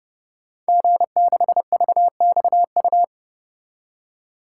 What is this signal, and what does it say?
Morse code, heard as a keyed tone, about 30 wpm, 700 Hz. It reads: G6VXU
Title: Code Groups - 5 Letter:Numbers - 30wpm